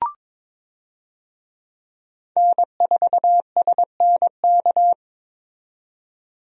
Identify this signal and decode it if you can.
Morse code, heard as a keyed tone, about 22 wpm, 700 Hz. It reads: N4SNK
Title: USA Call Signs - 5 Characters - 22wpm